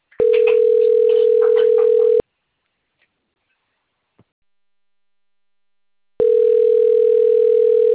Ringback tone (North America), 440 + 480 Hz.